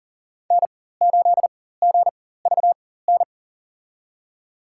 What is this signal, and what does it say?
Morse code, heard as a keyed tone, about 40 wpm, spaced as for 19 wpm, 700 Hz. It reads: N8GVN
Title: USA Call Signs - 5 Characters - 4x Character Spacing - 40wpm